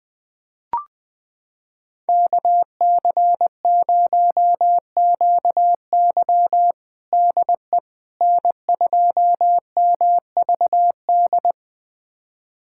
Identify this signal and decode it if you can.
Morse code, heard as a keyed tone, about 20 wpm, 700 Hz. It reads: KC0QY DE N2MVD
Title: USA Callsign DE Callsign - 5 Characters - 20wpm